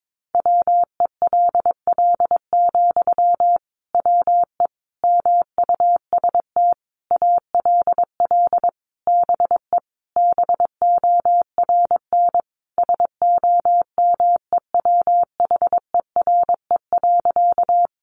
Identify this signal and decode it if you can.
Morse code, heard as a keyed tone, about 22 wpm, 700 Hz. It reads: WELL, WE MUST ALL BE BORN SOMEWHERE.
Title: Sentences from Top 1400 Words 22wpm